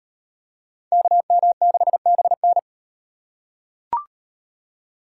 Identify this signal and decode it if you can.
Morse code, heard as a keyed tone, about 38 wpm, 700 Hz. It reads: KM6BN